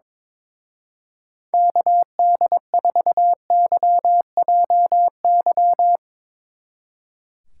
Morse code, heard as a keyed tone, about 23 wpm, 690 Hz. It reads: KD4YJY
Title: Speed Racing - USA Call Signs 22wpm